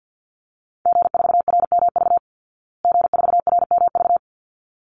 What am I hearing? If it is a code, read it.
Morse code, heard as a keyed tone, about 48 wpm, 700 Hz. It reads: G4RMV G4RMV
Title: Call Signs Repeated 3x - 50wpm